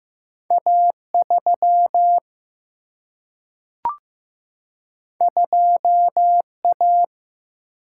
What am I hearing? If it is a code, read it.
Morse code, heard as a keyed tone, about 15 wpm, 700 Hz. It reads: A 3 2 A
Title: Single-Letter Single-Number 15wpm